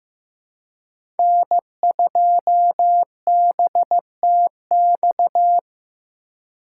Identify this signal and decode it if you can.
Morse code, heard as a keyed tone, about 15 wpm, 700 Hz. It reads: N2BTX
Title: K1USN SST QSO Practice - 15wpm